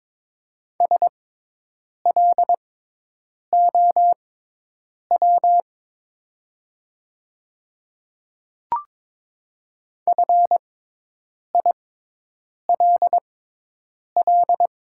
Morse code, heard as a keyed tone, about 22 wpm, spaced as for 8 wpm, 700 Hz. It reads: SLOW FILL
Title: 6x Character Spacing - Top 500 Words - 22wpm